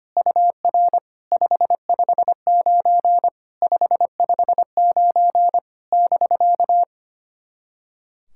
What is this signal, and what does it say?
Morse code, heard as a keyed tone, about 25 wpm, 700 Hz. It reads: UR 559 559 <BK>
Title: SOTA QSO Practice 25wpm